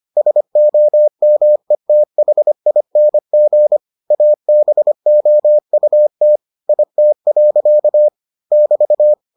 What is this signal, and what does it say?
Morse code, heard as a keyed tone, about 25 wpm, 600 Hz. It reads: SOMETHING ABOUT IT. <BT>